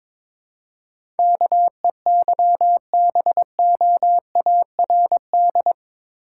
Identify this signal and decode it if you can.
Morse code, heard as a keyed tone, about 22 wpm, 700 Hz. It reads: KEYBOARD
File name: practice / 22wpm / Eight Letter Words - 22wpm